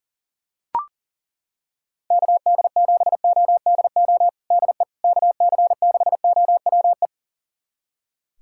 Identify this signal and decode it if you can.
Morse code, heard as a keyed tone, about 40 wpm, 700 Hz. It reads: KD7ODO DE KC6OWE
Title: USA Callsign DE Callsign - 6 Characters - 40wpm